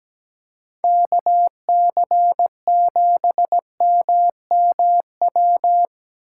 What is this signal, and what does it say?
Morse code, heard as a keyed tone, about 17 wpm, 700 Hz. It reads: KC7MMW
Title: USA Call Signs - 6 Characters - 8x Character Spacing - 17wpm